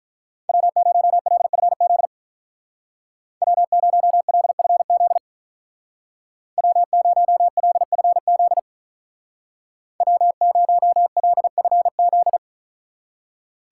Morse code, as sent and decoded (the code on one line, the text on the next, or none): .-- ----- .-.. ..-. --.. / .-- ----- .-.. ..-. --.. / .-- ----- .-.. ..-. --.. / .-- ----- .-.. ..-. --..
W0LFZ W0LFZ W0LFZ W0LFZ